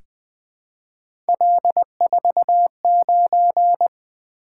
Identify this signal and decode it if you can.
Morse code, heard as a keyed tone, about 20 wpm, 700 Hz. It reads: L49
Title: Three Letter-Number Combinations - 20wpm